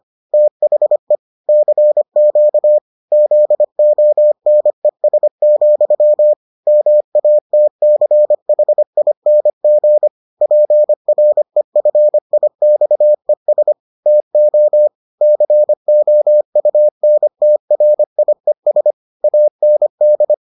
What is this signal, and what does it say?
Morse code, heard as a keyed tone, about 25 wpm, 600 Hz. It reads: THE CQ ZONES, MATCHING PREFIXES TO COUNTRIES AND